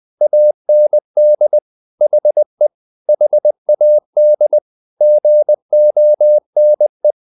Morse code, heard as a keyed tone, about 20 wpm, 600 Hz. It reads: AND HE HAD GONE
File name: the old man and the sea / the old man and the sea part 1 20wpm